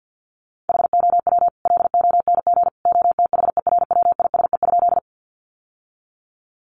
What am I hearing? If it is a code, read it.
Morse code, heard as a keyed tone, about 50 wpm, 700 Hz. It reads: HOW LONG OTHERWISE?
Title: Sentences from Top 1200 Words 50wpm